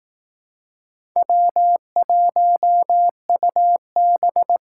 Morse code, heard as a keyed tone, about 18 wpm, 700 Hz. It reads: W1UB
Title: Speed Racing - USA Call Signs 15wpm